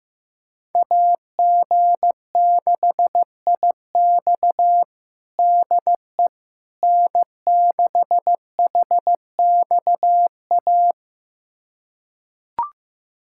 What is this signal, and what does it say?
Morse code, heard as a keyed tone, about 15 wpm, 700 Hz. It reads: AG6IX DE N6HXA
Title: USA Callsign DE Callsign - 5 Characters - 15wpm